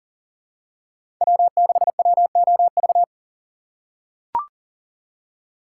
Morse code, heard as a keyed tone, about 40 wpm, 700 Hz. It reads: W6WOV
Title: USA Call Signs - 5 Characters - 4x Character Spacing - 40wpm